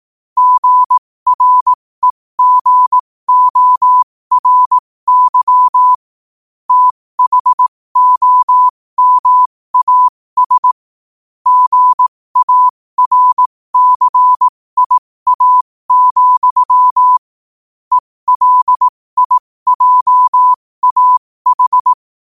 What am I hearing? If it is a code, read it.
Morse code, heard as a keyed tone, about 18 wpm, spaced as for 15 wpm, 1000 Hz. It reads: GREGORY THOMAS GARCIA, ELIJAH